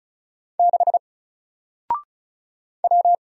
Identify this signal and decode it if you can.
Morse code, heard as a keyed tone, about 36 wpm, 700 Hz. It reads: 6 W
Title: Single Letter-Number Alt 35wpm